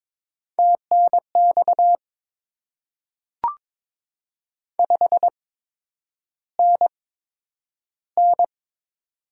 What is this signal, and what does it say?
Morse code, heard as a keyed tone, about 22 wpm, spaced as for 9 wpm, 700 Hz. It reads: TNX 5NN
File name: Top 20 QSO Elements - 8x Character Spacing - 22wpm